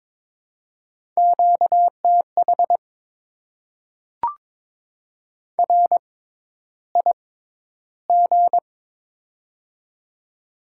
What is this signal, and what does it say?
Morse code, heard as a keyed tone, about 22 wpm, spaced as for 11 wpm, 700 Hz. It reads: QTH RIG
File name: Top 20 QSO Elements - 6x Character Spacing - 22wpm